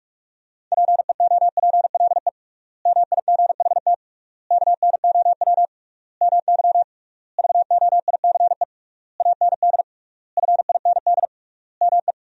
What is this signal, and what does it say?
Morse code, heard as a keyed tone, about 41 wpm, 700 Hz. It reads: PEOPLE MIGHT KNOW MY VOICE AND FIND ME
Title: 3x Word-Spacing - Sentences from Top 500 Words - 45wpm